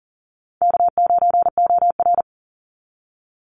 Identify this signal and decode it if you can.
Morse code, heard as a keyed tone, about 40 wpm, 700 Hz. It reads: K9OR
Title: Easy Call Signs Repeated 2x - 40wpm